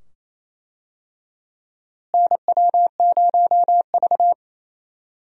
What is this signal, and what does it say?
Morse code, heard as a keyed tone, about 28 wpm, 700 Hz. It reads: NW0V
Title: USA Call Signs - 4 Characters - Repeated 2x - 28wpm